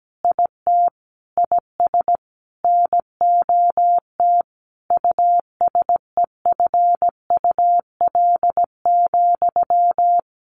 Morse code, heard as a keyed tone, about 17 wpm, 700 Hz. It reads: IT IS NOT USEFUL,
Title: Sentences from Top 1400 Words 17wpm